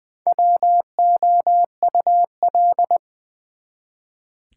Morse code, heard as a keyed tone, about 20 wpm, 700 Hz. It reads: WOUL
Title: Build A Word - 1 Letter At A Time - 3 to 5-Letter Words - 20wpm